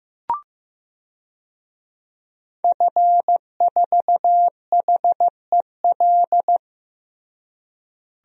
Morse code, heard as a keyed tone, about 15 wpm, 700 Hz. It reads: F4HEL
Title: DXCC-100 Call Signs - 5 Characters - Repeated 3x - 15wpm